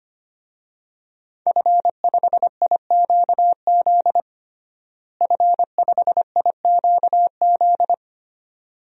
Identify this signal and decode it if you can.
Morse code, heard as a keyed tone, about 25 wpm, 700 Hz. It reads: F5IQZ F5IQZ
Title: DXCC-100 Call Signs - 5 Characters - Repeated 2x - 25wpm